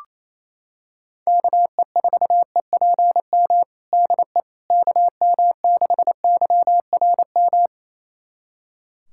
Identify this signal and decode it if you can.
Morse code, heard as a keyed tone, about 28 wpm, 700 Hz. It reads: KE4EPM DE KM6YRM
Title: USA Callsign DE Callsign - 6 Characters - 28wpm